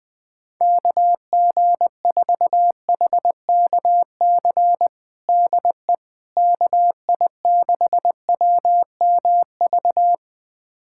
Morse code, heard as a keyed tone, about 20 wpm, 700 Hz. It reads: KG4HKC DE KI6WMV
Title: USA Callsign DE Callsign 20wpm